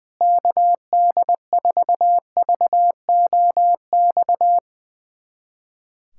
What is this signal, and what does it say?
Morse code, heard as a keyed tone, about 20 wpm, 700 Hz. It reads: KD4VOX
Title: USA Callsign DE Callsign - 6 Characters - 20wpm